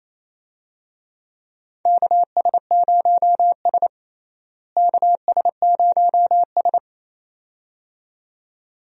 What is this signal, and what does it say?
Morse code, heard as a keyed tone, about 28 wpm, 700 Hz. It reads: KS0S KS0S
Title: USA Call Signs - 4 Characters - Repeated 2x - 28wpm